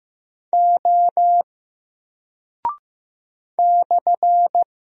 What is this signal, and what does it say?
Morse code, heard as a keyed tone, about 15 wpm, 700 Hz. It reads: O /
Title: Single Character (Letter:Number:Punctuation:Prosign) 15wpm